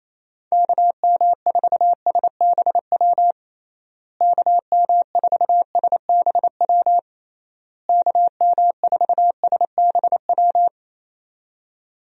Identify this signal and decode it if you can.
Morse code, heard as a keyed tone, about 28 wpm, 700 Hz. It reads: KM4SBW KM4SBW KM4SBW
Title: USA Call Signs - 6 Characters - Repeated 3x - 28wpm